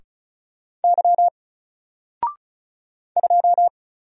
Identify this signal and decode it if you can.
Morse code, heard as a keyed tone, about 35 wpm, 700 Hz. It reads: Y 2
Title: Single Character (Letter:Number:Punctuation:Prosign) Alt 35wpm